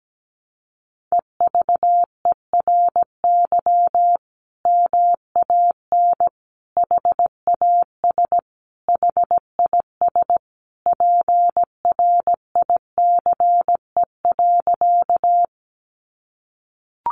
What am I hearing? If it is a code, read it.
Morse code, heard as a keyed tone, about 17 wpm, 700 Hz. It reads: EVERY MAN HAS HIS PRICE.
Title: Sentences from Top 600 Words 17wpm